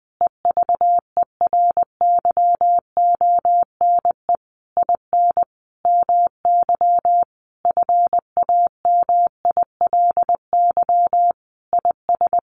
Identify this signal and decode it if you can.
Morse code, heard as a keyed tone, about 20 wpm, 700 Hz. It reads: EVERYONE IN MY FAMILY IS
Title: Sentences from Top 800 Words 20wpm